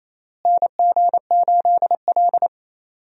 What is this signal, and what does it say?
Morse code, heard as a keyed tone, about 28 wpm, 700 Hz. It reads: NG8L
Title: USA Call Signs - 4 Characters - Repeated 3x - 28wpm